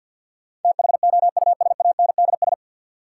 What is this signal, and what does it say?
Morse code, heard as a keyed tone, about 50 wpm, 700 Hz. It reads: THOUSANDS
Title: Sets of 4 Words - New Word Drill - 50wpm